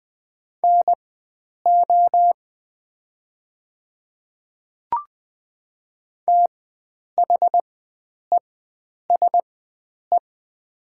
Morse code, heard as a keyed tone, about 20 wpm, spaced as for 9 wpm, 690 Hz. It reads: NO THESE
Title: 4x Character Spacing - Top 100 Words - 20wpm